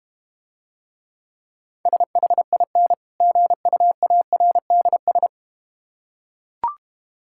Morse code, heard as a keyed tone, about 32 wpm, 700 Hz. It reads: SHIN GUARDS